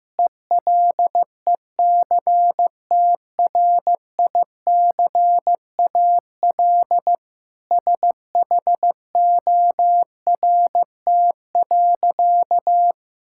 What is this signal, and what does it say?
Morse code, heard as a keyed tone, about 15 wpm, 700 Hz. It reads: ELECTRICAL SHORT.
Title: Sentences from Top 1000 Words - Review Part 6 - 15wpm